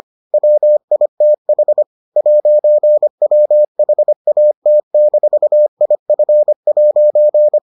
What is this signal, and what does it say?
Morse code, heard as a keyed tone, about 25 wpm, 600 Hz. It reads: WITH 'WHAT-IF'